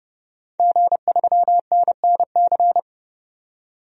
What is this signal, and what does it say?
Morse code, heard as a keyed tone, about 30 wpm, 700 Hz. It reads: G3NNC
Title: DXCC-100 Call Signs - 5 Characters - 30wpm